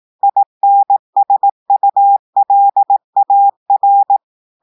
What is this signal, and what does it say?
Morse code, heard as a keyed tone, about 18 wpm, 800 Hz. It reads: INSULAR
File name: audios / portugais / portugais texte en clair 1 vitesse 18 mots minutes